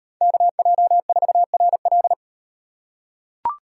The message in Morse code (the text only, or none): KJ4RL